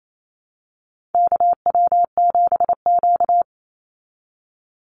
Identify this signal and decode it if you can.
Morse code, heard as a keyed tone, about 28 wpm, 700 Hz. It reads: KW7Q
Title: Call Signs Repeated 3x - 28wpm